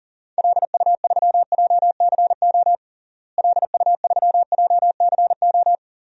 Morse code, heard as a keyed tone, about 40 wpm, 700 Hz. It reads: LU3JCO LU3JCO
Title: DXCC-100 Call Signs - 6 Characters - Repeated 3x - 40wpm